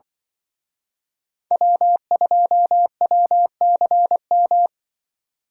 Morse code, heard as a keyed tone, about 24 wpm, 700 Hz. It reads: W2WCM
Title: Speed Racing - USA 1x3 Call Signs 20wpm